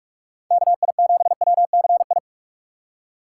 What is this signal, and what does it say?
Morse code, heard as a keyed tone, about 45 wpm, 700 Hz. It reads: KI7WCI